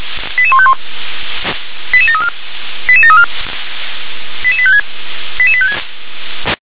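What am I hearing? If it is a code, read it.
Five-tone selective call (ZVEI), heard as quick runs of five tones, tones of about 70 ms. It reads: 88151 88734 88734 88856 88856
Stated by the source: ZVEI five-tone selective call received on an SDR.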